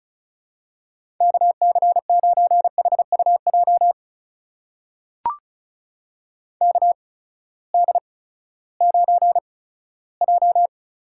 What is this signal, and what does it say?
Morse code, heard as a keyed tone, about 34 wpm, spaced as for 17 wpm, 700 Hz. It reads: KC9HUJ KD9J
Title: USA Call Signs - 6 Characters - 8x Character Spacing - 35wpm